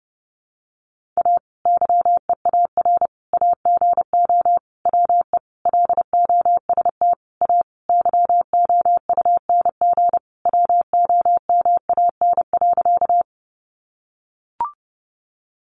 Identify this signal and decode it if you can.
Morse code, heard as a keyed tone, about 30 wpm, 700 Hz. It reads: A YEAR AGO WE LOST A YOUNG WOMAN.
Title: Sentences from Top 1400 Words 30wpm